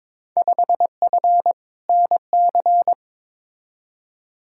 Morse code, heard as a keyed tone, about 22 wpm, 700 Hz. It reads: 5F NC
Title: Field Day Exchanges Encore - 22wpm